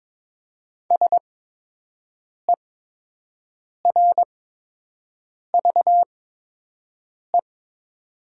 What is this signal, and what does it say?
Morse code, heard as a keyed tone, about 22 wpm, spaced as for 6 wpm, 700 Hz. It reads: SERVE
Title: 8x Character Spacing - Top 500 Words - 22wpm